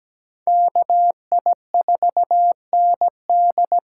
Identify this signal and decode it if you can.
Morse code, heard as a keyed tone, about 17 wpm, 700 Hz. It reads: KI4ND
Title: USA Call Signs - 6 Characters - 8x Character Spacing - 17wpm